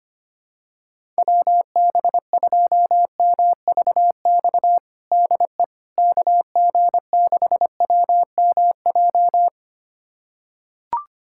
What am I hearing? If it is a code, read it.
Morse code, heard as a keyed tone, about 25 wpm, 700 Hz. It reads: WB2MVX DE KG6WMJ
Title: USA Callsign DE Callsign - 6 Characters - 25wpm